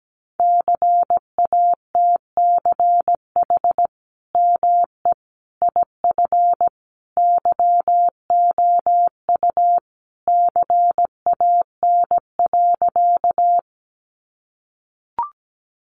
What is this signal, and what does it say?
Morse code, heard as a keyed tone, about 17 wpm, 700 Hz. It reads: CATCH ME IF YOU CAN.
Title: Sentences from Top 1500 Words 17wpm